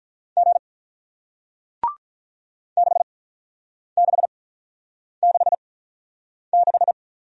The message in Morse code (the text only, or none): G 6 6 6 6